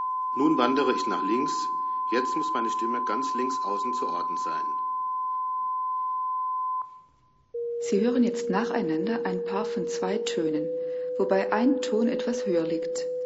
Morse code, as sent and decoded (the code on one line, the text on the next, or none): none